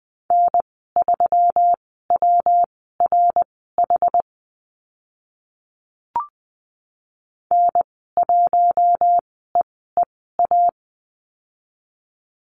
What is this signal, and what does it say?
Morse code, heard as a keyed tone, about 20 wpm, spaced as for 14 wpm, 700 Hz. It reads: N3WRH N1EEA
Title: 128 Lesson 21g - Number 6 - Call Signs 20wpm